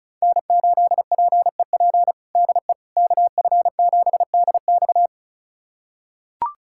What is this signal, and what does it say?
Morse code, heard as a keyed tone, about 36 wpm, 700 Hz. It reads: N8PEP DE KF7DX
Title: USA Callsign DE Callsign - 5 Characters - 35wpm